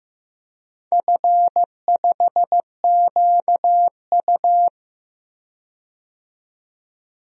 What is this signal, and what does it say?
Morse code, heard as a keyed tone, about 15 wpm, 700 Hz. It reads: F5QU
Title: DXCC-100 Call Signs - 4 Characters - 15wpm